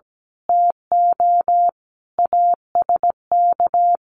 Morse code, heard as a keyed tone, about 17 wpm, 700 Hz. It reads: TO ASK
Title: Sentences from Top 1500 Words 17wpm